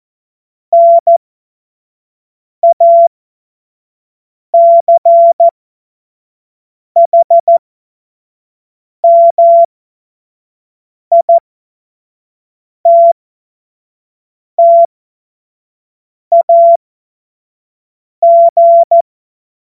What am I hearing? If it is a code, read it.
Morse code, heard as a keyed tone, about 14 wpm, spaced as for 5 wpm, 680 Hz. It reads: NACHMITTAG
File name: Morsekurs / Witze / Witz 06. Dezember 2022 25 F70